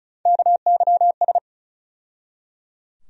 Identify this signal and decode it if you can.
Morse code, heard as a keyed tone, about 36 wpm, 700 Hz. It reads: KYS
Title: USA Call Sign Postfixes - 3 Letters 35wpm